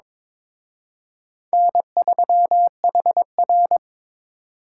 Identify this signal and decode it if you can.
Morse code, heard as a keyed tone, about 26 wpm, 700 Hz. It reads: N3HR
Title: Speed Racing - USA 1x2 Call Signs 22wpm